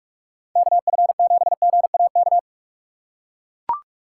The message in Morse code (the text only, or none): KF7GAK